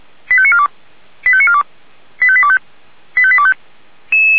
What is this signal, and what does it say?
Five-tone selective call (ZVEI), heard as quick runs of five tones, tones of about 70 ms. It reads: 85942 85942 85825 85825